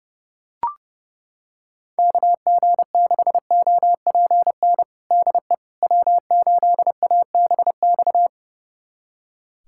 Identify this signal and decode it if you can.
Morse code, heard as a keyed tone, about 30 wpm, 700 Hz. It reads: KG6OPN DE W8ABX
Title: USA Callsign DE Callsign 30wpm